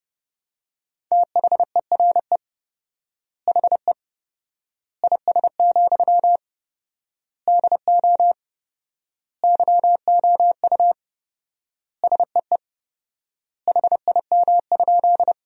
Morse code, heard as a keyed tone, about 32 wpm, 700 Hz. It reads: THERE HE IS, DO YOU SEE HIM?
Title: 4x Word-Spacing - Sentences from Top 100 Words - 30wpm